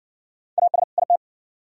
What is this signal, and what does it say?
Morse code, heard as a keyed tone, about 58 wpm, 700 Hz. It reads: AS IT